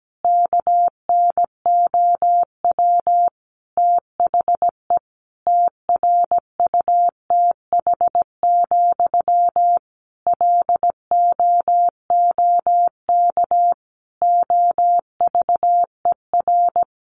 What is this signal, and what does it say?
Morse code, heard as a keyed tone, about 17 wpm, 700 Hz. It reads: KNOW THE TRUTH, LOOK OVER
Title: Sentences from Top 800 Words 17wpm